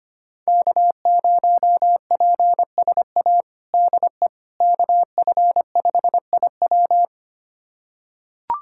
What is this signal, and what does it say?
Morse code, heard as a keyed tone, about 25 wpm, 700 Hz. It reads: K0PSA DE KF5IW
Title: USA Callsign DE Callsign - 5 Characters - 25wpm